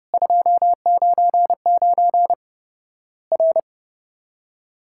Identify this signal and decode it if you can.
Morse code, heard as a keyed tone, about 30 wpm, 700 Hz. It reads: 299 R
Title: Worked All Europe DX Contest - QTC Exchange - 30wpm